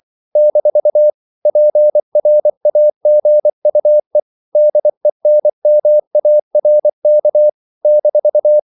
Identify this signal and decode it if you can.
Morse code, heard as a keyed tone, about 24 wpm, 600 Hz. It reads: - PRAGUE DENMARK -
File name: CW / Practice / NationalCapitals24